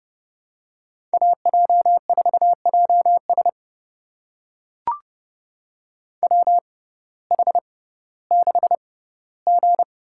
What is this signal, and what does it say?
Morse code, heard as a keyed tone, about 30 wpm, spaced as for 16 wpm, 700 Hz. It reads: AJ4JS WH6G
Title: USA Call Signs - 5 Characters - 6x Character Spacing - 30wpm